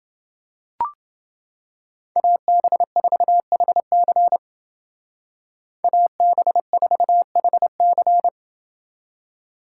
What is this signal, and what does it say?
Morse code, heard as a keyed tone, about 30 wpm, 700 Hz. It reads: AB4HC AB4HC